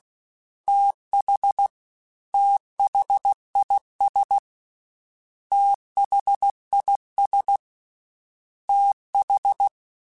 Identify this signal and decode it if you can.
Morse code, heard as a keyed tone, about 16 wpm, 800 Hz. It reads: TH THIS THIS TH
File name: THs / TH / TH Prefixes 3x2 @15